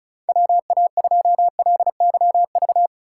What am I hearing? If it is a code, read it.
Morse code, heard as a keyed tone, about 34 wpm, 700 Hz. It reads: WA2LYV
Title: USA Call Signs - 3x Character Spacing - 35wpm